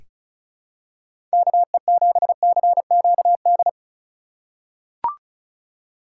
Morse code, heard as a keyed tone, about 36 wpm, 700 Hz. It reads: KE7CQD
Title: USA Call Signs - 6 Characters - 2x Character Spacing - 35wpm